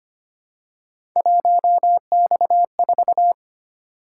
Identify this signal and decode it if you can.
Morse code, heard as a keyed tone, about 25 wpm, 700 Hz. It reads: 1X4
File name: Three Letter-Number Combinations - 25wpm